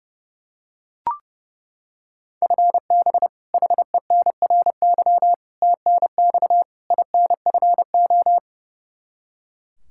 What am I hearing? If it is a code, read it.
Morse code, heard as a keyed tone, about 30 wpm, 700 Hz. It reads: FB HENRY TNX INFO
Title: Practice QSO - Exchange 3 - More Info - 30wpm